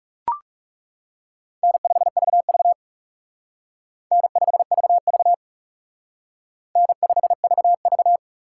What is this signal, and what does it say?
Morse code, heard as a keyed tone, about 36 wpm, 700 Hz. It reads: N5VV N5VV N5VV
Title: Speed Racing - USA 1x2 Call Signs 30wpm